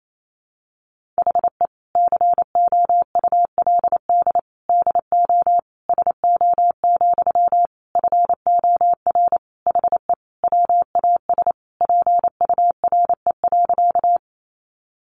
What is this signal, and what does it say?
Morse code, heard as a keyed tone, about 28 wpm, 700 Hz. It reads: HE COULD DO SO, FOR HE WAS PURE.
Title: Sentences from Top 1500 Words 28wpm